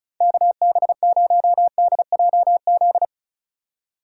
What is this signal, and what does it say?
Morse code, heard as a keyed tone, about 36 wpm, 700 Hz. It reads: KB0DJZ
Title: USA Call Signs - 6 Characters - 2x Character Spacing - 35wpm